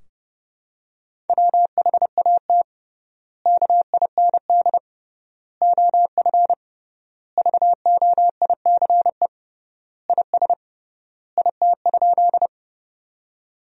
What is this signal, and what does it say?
Morse code, heard as a keyed tone, about 30 wpm, 700 Hz. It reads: WHAT KIND OF VOICE IS IT?
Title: 3x Word-Spacing - Sentences from Top 500 Words - 30wpm